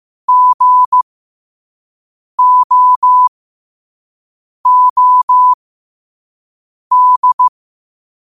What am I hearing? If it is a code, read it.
Morse code, heard as a keyed tone, about 15 wpm, spaced as for 5 wpm, 1000 Hz. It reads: GOOD
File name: qotdfiles / 5wpm